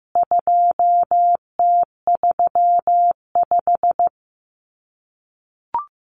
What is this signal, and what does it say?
Morse code, heard as a keyed tone, about 15 wpm, 700 Hz. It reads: 2T35
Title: CWT Exchange 15wpm